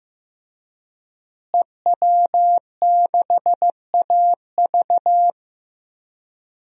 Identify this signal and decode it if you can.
Morse code, heard as a keyed tone, about 16 wpm, 700 Hz. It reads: EW6AV
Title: DXCC-100 Call Signs - 5 Characters - Repeated 2x - 15wpm